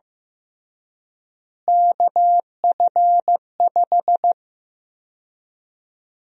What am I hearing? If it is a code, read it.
Morse code, heard as a keyed tone, about 15 wpm, 700 Hz. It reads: KF5
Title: USA Call Sign Prefixes - Repeated 2x - 15wpm